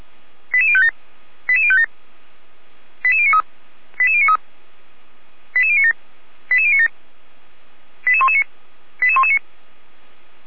Five-tone selective call (ZVEI), heard as quick runs of five tones, tones of about 70 ms. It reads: 88057 88057 88093 88093 88097 88097 88119 88119
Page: audio.f85360.feuerwehr-orlamuende.org